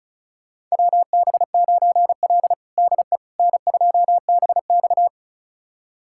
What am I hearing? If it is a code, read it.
Morse code, heard as a keyed tone, about 34 wpm, 700 Hz. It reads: WB9L DE N2BX
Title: USA Callsign DE Callsign - 4 Characters - 35wpm